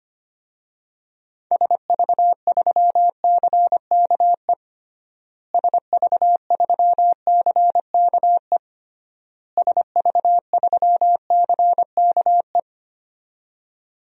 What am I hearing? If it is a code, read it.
Morse code, heard as a keyed tone, about 25 wpm, 700 Hz. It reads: SV3CKE SV3CKE SV3CKE
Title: DXCC-100 Call Signs - 6 Characters - Repeated 3x - 25wpm